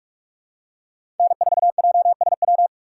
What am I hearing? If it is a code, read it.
Morse code, heard as a keyed tone, about 45 wpm, 700 Hz. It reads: N4JSW
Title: USA Call Signs - 2x Character Spacing - 45wpm